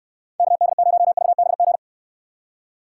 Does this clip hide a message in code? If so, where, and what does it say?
Yes: Morse code from 0.4 to 1.8 s, heard as a keyed tone, about 69 wpm, 700 Hz. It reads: KD0VBG